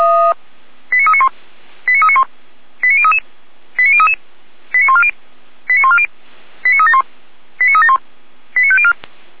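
Five-tone selective call (ZVEI), heard as quick runs of five tones, tones of about 70 ms. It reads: 89391 89391 89033 89033 89140 89140 89371 89371 89504
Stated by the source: SDR reception of ZVEI five-tone call 89079, from the feuerwehr-orlamuende.org fire brigade server.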